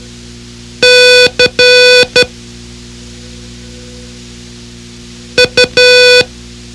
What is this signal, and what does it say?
Morse code, heard as a keyed tone, about 12 wpm, spaced as for 3 wpm, 500 Hz. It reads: CU
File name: ham radio / CodeQuick / Code Quick 04